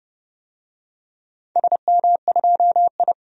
Morse code, heard as a keyed tone, about 30 wpm, 700 Hz. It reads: SM2I
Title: DXCC-100 Call Signs - 4 Characters - 30wpm